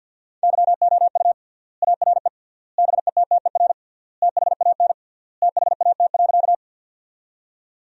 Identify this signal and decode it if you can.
Morse code, heard as a keyed tone, about 50 wpm, 700 Hz. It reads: YOU ARE BETTER THAN THAT.
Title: 3x Word-Spacing - Sentences from Top 400 Words - 50wpm